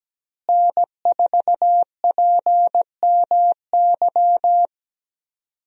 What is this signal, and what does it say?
Morse code, heard as a keyed tone, about 17 wpm, 700 Hz. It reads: N4PMY
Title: USA Call Signs - 5 Characters - Repeated 2x - 17wpm